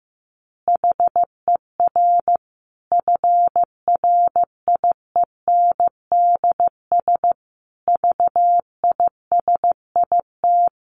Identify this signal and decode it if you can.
Morse code, heard as a keyed tone, about 15 wpm, 700 Hz. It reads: HER FRIENDS VISIT
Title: Easy Phrases 15wpm